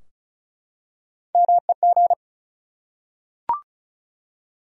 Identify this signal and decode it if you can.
Morse code, heard as a keyed tone, about 36 wpm, 700 Hz. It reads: MEG
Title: USA Common Names - Max 4 Letters - 35wpm